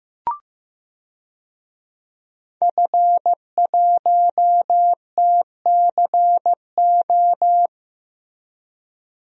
Morse code, heard as a keyed tone, about 15 wpm, 700 Hz. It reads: F1TCO
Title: DXCC-100 Call Signs - 5 Characters - Repeated 3x - 15wpm